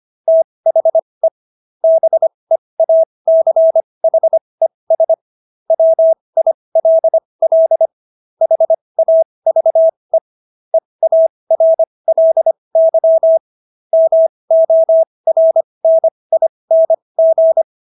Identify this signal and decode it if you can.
Morse code, heard as a keyed tone, about 25 wpm, spaced as for 20 wpm, 650 Hz. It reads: THE BEACHES WILL HAVE EARLY MORNING